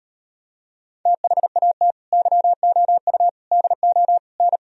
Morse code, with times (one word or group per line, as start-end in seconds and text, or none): 1.1-1.9 THAT
2.1-3.3 YOU
3.5-4.2 DO
4.4-4.6 N